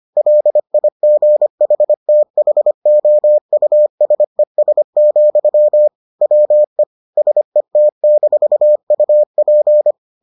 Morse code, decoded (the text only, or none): LIGHTHOUSES, WE SET-UP